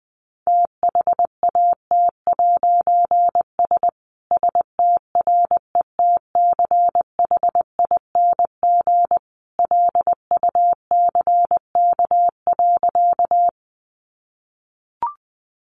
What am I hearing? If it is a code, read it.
Morse code, heard as a keyed tone, about 20 wpm, 700 Hz. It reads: THAT'S STRETCHING LUCK.
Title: A Sound Of Thunder 20wpm